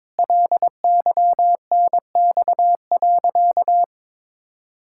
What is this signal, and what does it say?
Morse code, heard as a keyed tone, about 22 wpm, 700 Hz. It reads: LYNX.
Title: Pangram Challenge v2 - 22wpm